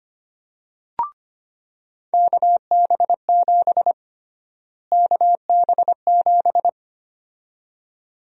Morse code, heard as a keyed tone, about 25 wpm, 700 Hz. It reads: KB7 KB7
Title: USA Call Sign Prefixes - Repeated 2x - 25wpm